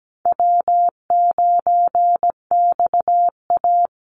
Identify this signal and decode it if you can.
Morse code, heard as a keyed tone, about 17 wpm, 700 Hz. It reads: W9XA
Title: Easy Call Signs Repeated 2x - 17wpm